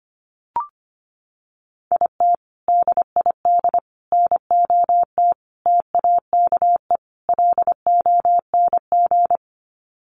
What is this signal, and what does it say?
Morse code, heard as a keyed tone, about 25 wpm, 700 Hz. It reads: IT DID NOT TAKE LONG
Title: Sets of 5 Words Encore2 25wpm